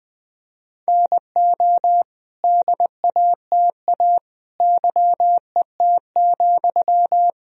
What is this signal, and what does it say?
Morse code, heard as a keyed tone, about 21 wpm, 700 Hz. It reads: NO DATA YET,
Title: Sentences from Top 1000 Words - Review Part 3 - 20wpm